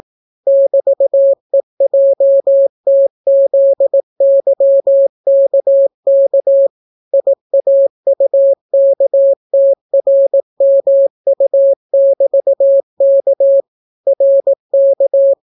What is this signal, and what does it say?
Morse code, heard as a keyed tone, about 18 wpm, 550 Hz. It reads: <BT>EJTZYKK IAUKTRMU<BT>K RK